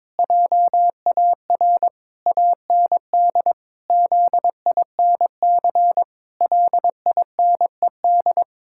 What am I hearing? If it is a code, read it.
Morse code, heard as a keyed tone, about 22 wpm, 700 Hz. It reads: JAR AND ZINC LINED
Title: Pangram Challenge v2 - 22wpm